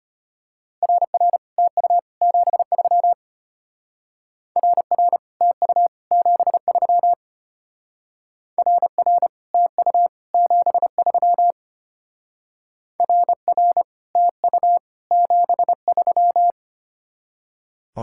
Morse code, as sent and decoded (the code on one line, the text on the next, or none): .-. .-. / - ..- / --... ...-- / .-. .-. / - ..- / --... ...-- / .-. .-. / - ..- / --... ...-- / .-. .-. / - ..- / --... ...--
RR TU 73 RR TU 73 RR TU 73 RR TU 73